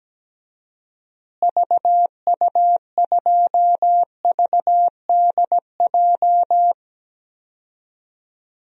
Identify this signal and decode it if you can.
Morse code, heard as a keyed tone, about 17 wpm, 700 Hz. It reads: VU2VDJ
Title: DXCC-100 Call Signs - 6 Characters - Repeated 2x - 17wpm